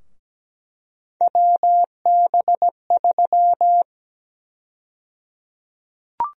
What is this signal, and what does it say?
Morse code, heard as a keyed tone, about 17 wpm, 700 Hz. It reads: WB3
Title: USA Call Sign Prefixes - Repeated 2x - 17wpm